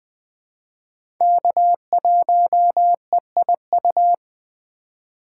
Morse code, heard as a keyed tone, about 20 wpm, 700 Hz. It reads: K1EIU
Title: USA Call Signs - 2x Character Spacing - 20wpm